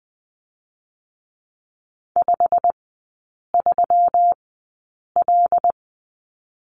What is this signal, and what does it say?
Morse code, heard as a keyed tone, about 20 wpm, spaced as for 8 wpm, 700 Hz. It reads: 53L
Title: 205 Lesson 33e - Letter j - Three Characters 20wpm